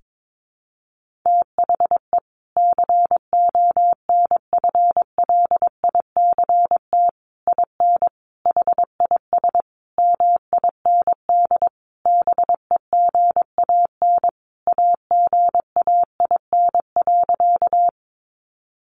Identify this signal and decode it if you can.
Morse code, heard as a keyed tone, about 22 wpm, 700 Hz. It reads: THE CONFLICT IN HIS MIND BEGAN AGAIN.
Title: Sentences from Top 1300 Words 22wpm